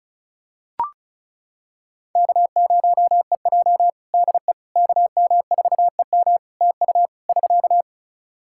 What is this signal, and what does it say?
Morse code, heard as a keyed tone, about 36 wpm, 700 Hz. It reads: K0EJ DE KM4EM TU <SK>
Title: Practice QSO - Exchange 4 - Ending - 35wpm